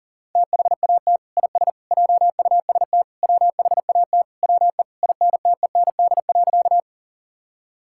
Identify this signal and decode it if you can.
Morse code, heard as a keyed tone, about 38 wpm, 700 Hz. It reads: THAT IS JUST WHAT WE INTEND.
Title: Sentences from Top 1600 Words 40wpm